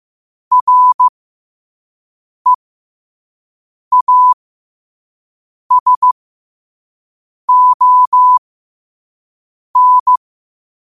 Morse code, heard as a keyed tone, about 15 wpm, spaced as for 5 wpm, 1000 Hz. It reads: REASON
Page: qotd-at-5-wpm